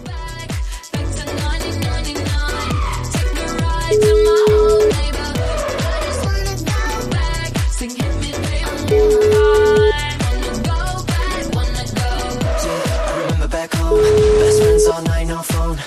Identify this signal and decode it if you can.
Ringback tone (Europe, 425 Hz), with noise.